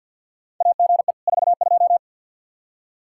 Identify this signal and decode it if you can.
Morse code, heard as a keyed tone, about 47 wpm, 700 Hz. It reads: AGE 42
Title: Practice QSO - Exchange 3 - More Info - 50wpm